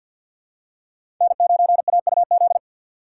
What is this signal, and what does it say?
Morse code, heard as a keyed tone, about 53 wpm, 700 Hz. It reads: N9AUZ
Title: USA Call Signs - 5 Characters - 3x Character Spacing - 50wpm